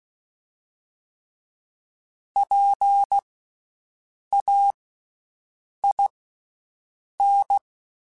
Morse code, heard as a keyed tone, about 16 wpm, spaced as for 6 wpm, 800 Hz. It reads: PAIN
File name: Special INGs / Special INGs-mp3s / VerySlowToFaster Ing @15